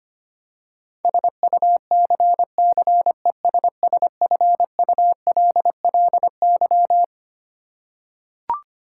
Morse code, heard as a keyed tone, about 25 wpm, 700 Hz. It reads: SUCCESSFULLY
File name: practice / 25wpm / Twelve Letter Words - 25wpm